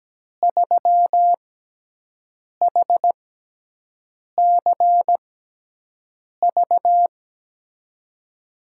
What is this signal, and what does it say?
Morse code, heard as a keyed tone, about 17 wpm, spaced as for 6 wpm, 700 Hz. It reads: 3HCV
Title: USA Call Signs - 5 Characters - 6x Character Spacing - 17wpm